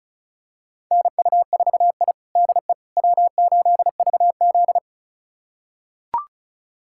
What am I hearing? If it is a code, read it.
Morse code, heard as a keyed tone, about 36 wpm, 700 Hz. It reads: NU4I DE W8VZ
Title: USA Callsign DE Callsign - 4 Characters - 35wpm